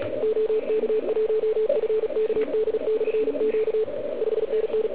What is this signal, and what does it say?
Morse code, heard as a keyed tone, about 36 wpm, 440 Hz, with radio noise. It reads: OK1FKD/Q 5NN